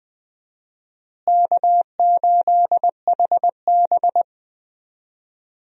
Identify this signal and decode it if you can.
Morse code, heard as a keyed tone, about 21 wpm, 700 Hz. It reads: K8HB
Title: USA Call Signs - 4 Characters - Repeated 2x - 20wpm